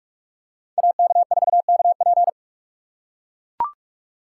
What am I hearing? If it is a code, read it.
Morse code, heard as a keyed tone, about 43 wpm, 700 Hz. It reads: AK4KP